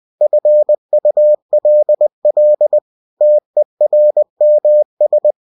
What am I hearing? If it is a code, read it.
Morse code, heard as a keyed tone, about 20 wpm, 600 Hz. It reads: FULL TERMS
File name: cw-books / winnie-the-pooh / winnie-20wpm-page-0113